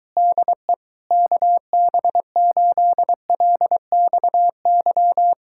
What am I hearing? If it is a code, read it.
Morse code, heard as a keyed tone, about 23 wpm, 700 Hz. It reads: DE KB8LXY